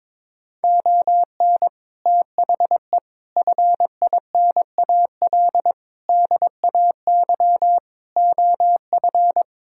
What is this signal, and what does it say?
Morse code, heard as a keyed tone, about 22 wpm, 700 Hz. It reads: ON THE FINAL DAY OF
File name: practice / 22wpm / Sets of 6 Words - 22wpm